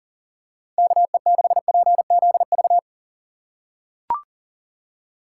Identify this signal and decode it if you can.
Morse code, heard as a keyed tone, about 40 wpm, 700 Hz. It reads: KE6PZV
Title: USA Call Signs - 6 Characters - 2x Character Spacing - 40wpm